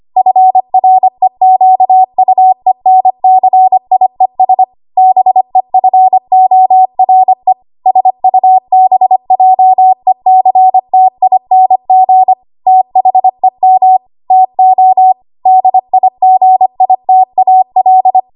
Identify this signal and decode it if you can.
Morse code, heard as a keyed tone, about 25 wpm, 750 Hz. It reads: FREQUENCIES BEFORE SUBJECTING THEM TO DIGITAL
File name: Archive / 25 WPM / 25WPM